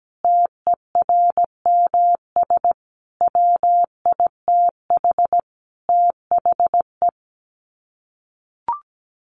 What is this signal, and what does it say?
Morse code, heard as a keyed tone, about 17 wpm, 700 Hz. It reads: TERMS WITH THE
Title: Sets of 5 Words Encore2 17wpm